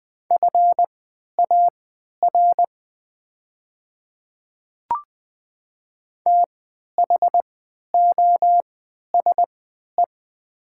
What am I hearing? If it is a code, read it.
Morse code, heard as a keyed tone, about 20 wpm, spaced as for 11 wpm, 700 Hz. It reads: FAR THOSE
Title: 3x Character Spacing - Top 300 Words - 20wpm